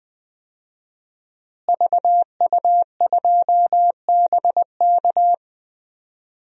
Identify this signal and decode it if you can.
Morse code, heard as a keyed tone, about 21 wpm, 700 Hz. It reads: VU2BK
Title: DXCC-100 Call Signs - 5 Characters - Repeated 3x - 20wpm